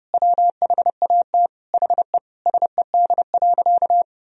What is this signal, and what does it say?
Morse code, heard as a keyed tone, about 30 wpm, 700 Hz. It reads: WHAT HE SED.
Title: Sentences from Top 400 Words and Abbreviations 30wpm